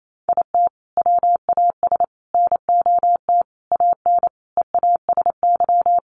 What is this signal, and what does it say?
Morse code, heard as a keyed tone, about 28 wpm, 700 Hz. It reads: IT WAS NOT AN EASY
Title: Sentences from Top 1300 Words 28wpm